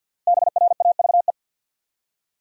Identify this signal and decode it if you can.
Morse code, heard as a keyed tone, about 53 wpm, 700 Hz. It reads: BRAVE
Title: Five Letter Words - 50wpm